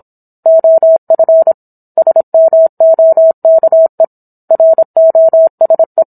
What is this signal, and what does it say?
Morse code, heard as a keyed tone, about 27 wpm, 650 Hz. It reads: OF SMOKE ROSE
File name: Book One / 17-Chapter 17 - 26 WPM